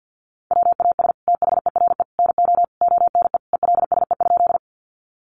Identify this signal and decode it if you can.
Morse code, heard as a keyed tone, about 50 wpm, 700 Hz. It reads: WAS THERE NO ONE ELSE?
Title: Sentences from Top 800 Words 50wpm